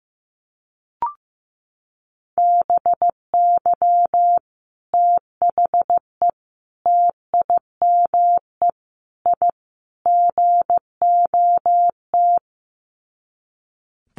Morse code, heard as a keyed tone, about 15 wpm, 700 Hz. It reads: BY THE TIME I GOT
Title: Sets of 5 Words Encore2 15wpm